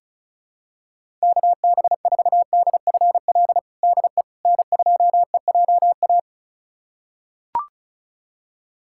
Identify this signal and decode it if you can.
Morse code, heard as a keyed tone, about 34 wpm, 700 Hz. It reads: KB4DFL DE N2EJA
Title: USA Callsign DE Callsign 35wpm